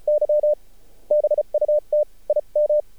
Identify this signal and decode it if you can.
Morse code, heard as a keyed tone, about 37 wpm, spaced as for 32 wpm, 600 Hz. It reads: Y BUT I M
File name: archived / CRSnet